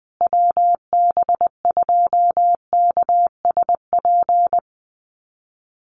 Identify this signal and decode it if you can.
Morse code, heard as a keyed tone, about 20 wpm, 700 Hz. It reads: WB2KSP
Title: USA Call Signs Repeated 3x 20wpm